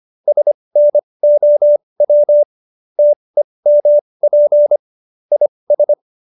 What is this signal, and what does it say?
Morse code, heard as a keyed tone, about 25 wpm, spaced as for 20 wpm, 590 Hz. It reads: SNOW TEMP IS